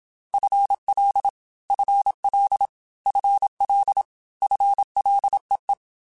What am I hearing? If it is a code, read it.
Morse code, heard as a keyed tone, about 27 wpm, 800 Hz. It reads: FL FL FL FLEE
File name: FL / FL mp3s / FL Prefixes 3x2 @25